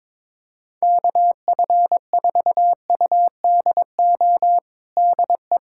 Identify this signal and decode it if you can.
Morse code, heard as a keyed tone, about 22 wpm, 700 Hz. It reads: KF4UDO DE
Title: USA Callsign DE Callsign 22wpm